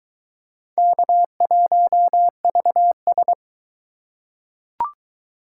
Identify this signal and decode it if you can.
Morse code, heard as a keyed tone, about 20 wpm, 700 Hz. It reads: K1VS